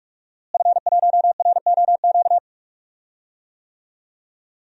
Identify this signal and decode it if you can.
Morse code, heard as a keyed tone, about 46 wpm, 700 Hz. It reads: F1ROQ
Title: DXCC-100 Call Signs - 5 Characters - Repeated 3x - 45wpm